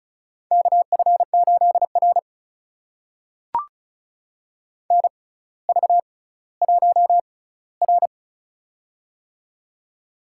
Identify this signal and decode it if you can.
Morse code, heard as a keyed tone, about 36 wpm, spaced as for 18 wpm, 700 Hz. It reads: KF8R NV1R
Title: USA Call Signs - 4 Characters - 6x Character Spacing - 35wpm